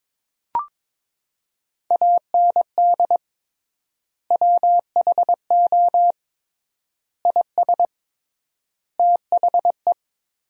Morse code, heard as a keyed tone, about 22 wpm, 700 Hz. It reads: AND WHO IS THE
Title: 3x Word-Spacing - Sentences from Top 200 Words - 22wpm